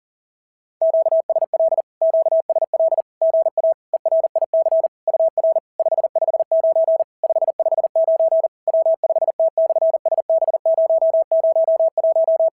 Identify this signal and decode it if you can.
Morse code, heard as a keyed tone, about 40 wpm, 690 Hz. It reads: QSL QSL GA ERIC UR 559 559 W5T/SB001
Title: SOTA QSO Practice 40wpm